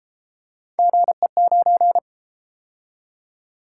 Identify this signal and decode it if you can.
Morse code, heard as a keyed tone, about 32 wpm, 700 Hz. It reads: GE9